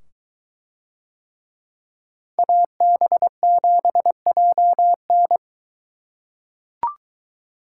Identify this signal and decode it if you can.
Morse code, heard as a keyed tone, about 23 wpm, 700 Hz. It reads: AB7JN